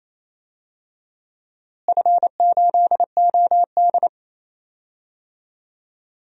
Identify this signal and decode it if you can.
Morse code, heard as a keyed tone, about 28 wpm, 700 Hz. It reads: F8OD
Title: DXCC-100 Call Signs - 4 Characters - Repeated 2x - 28wpm